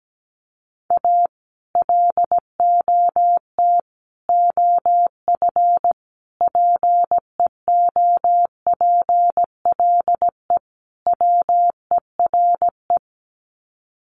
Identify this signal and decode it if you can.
Morse code, heard as a keyed tone, about 17 wpm, 700 Hz. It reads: A LOT OF PEOPLE WERE
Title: Sets of 5 Words Encore2 17wpm